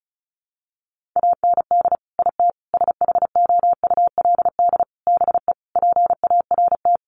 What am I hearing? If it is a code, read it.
Morse code, heard as a keyed tone, about 34 wpm, 700 Hz. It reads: AND IT SHOULD BE PART
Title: Sentences from Top 1500 Words 35wpm